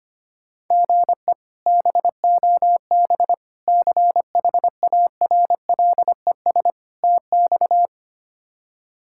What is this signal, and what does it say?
Morse code, heard as a keyed tone, about 25 wpm, 700 Hz. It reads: GE BOB CHARLES TX
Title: K1USN SST QSO Practice - 25wpm